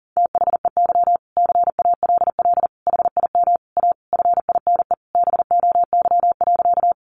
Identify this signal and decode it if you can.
Morse code, heard as a keyed tone, about 40 wpm, 700 Hz. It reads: THEY CALL HIM A FINE BOY.
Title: Sentences from Top 500 Words 40wpm